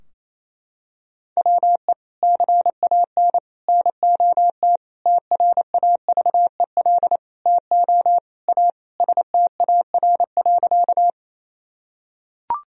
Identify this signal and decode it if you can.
Morse code, heard as a keyed tone, about 28 wpm, 700 Hz. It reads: WE CAN NOT TRAVEL TO A STAR.
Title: Sentences from Top 1000 Words - Review Part 2 - 28wpm